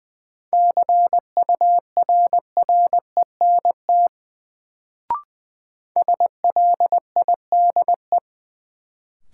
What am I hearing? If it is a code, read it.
Morse code, heard as a keyed tone, about 20 wpm, 700 Hz. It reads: CURRENT SLIDE
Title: Sentences from Top 1600 Words - New Word Drill Alt - 20wpm